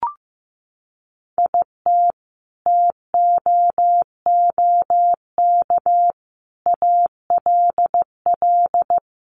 Morse code, heard as a keyed tone, about 15 wpm, 700 Hz. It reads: IT TOOK ALL
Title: Sentences from Top 300 Words 15wpm